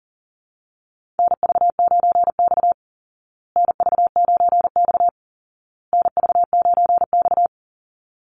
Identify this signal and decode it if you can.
Morse code, heard as a keyed tone, about 40 wpm, 700 Hz. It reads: NV9X NV9X NV9X
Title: Call Signs Repeated 2x - 40wpm